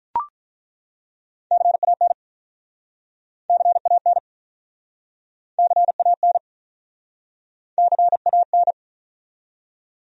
Morse code, heard as a keyed tone, about 48 wpm, 700 Hz. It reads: CAN CAN CAN CAN